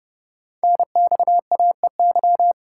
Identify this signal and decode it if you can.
Morse code, heard as a keyed tone, about 30 wpm, 700 Hz. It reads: NXAEY
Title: Code Groups - 5 Letters - 30wpm